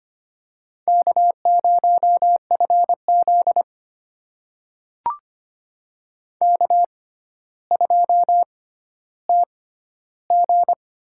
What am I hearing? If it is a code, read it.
Morse code, heard as a keyed tone, about 25 wpm, spaced as for 13 wpm, 700 Hz. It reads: K0FZ K2TG
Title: USA Call Signs - 4 Characters - 6x Character Spacing - 25wpm